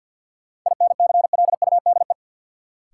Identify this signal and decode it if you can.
Morse code, heard as a keyed tone, about 48 wpm, 700 Hz. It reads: INCLUDE
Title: Sentences from Top 1000 Words - Review Part 3 - New Word Drill - 50wpm